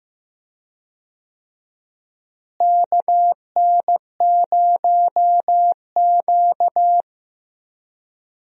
Morse code, heard as a keyed tone, about 15 wpm, 700 Hz. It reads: KN0Q
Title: USA Call Signs - 4 Characters - Repeated 3x - 15wpm